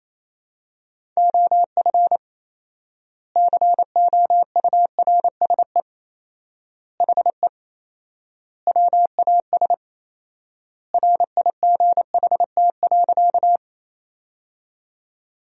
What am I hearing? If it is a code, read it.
Morse code, heard as a keyed tone, about 28 wpm, 700 Hz. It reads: OF COURSE HE WAS RIGHT.
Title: 4x Word-Spacing - Sentences from Top 500 Words - 28wpm